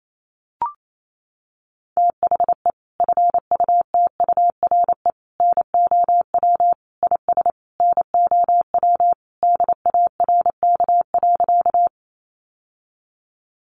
Morse code, heard as a keyed tone, about 28 wpm, 700 Hz. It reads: THE FUTURE NOW IS NOW DARK.
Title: Sentences from Top 700 Words 28wpm